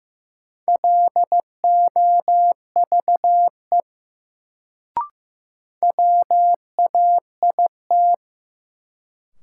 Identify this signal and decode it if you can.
Morse code, heard as a keyed tone, about 15 wpm, 700 Hz. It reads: LOVE WAIT
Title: Sets of 5 Words Encore - New Word Drill - 15wpm